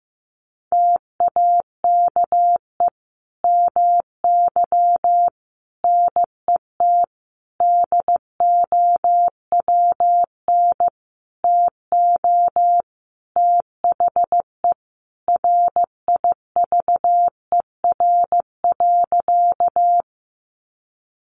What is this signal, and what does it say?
Morse code, heard as a keyed tone, about 15 wpm, 700 Hz. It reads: TAKE MY NET DOWN TO THE RIVER.
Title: Sentences from Top 1400 Words 15wpm